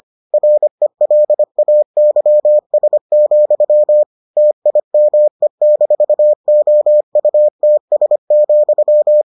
Morse code, heard as a keyed tone, about 25 wpm, 600 Hz. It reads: RELAYS, TIME-OUTS,